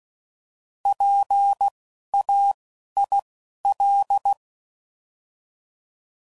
Morse code, heard as a keyed tone, about 16 wpm, spaced as for 11 wpm, 800 Hz. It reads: PAIL
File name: Prefixes / By First Letter / Ps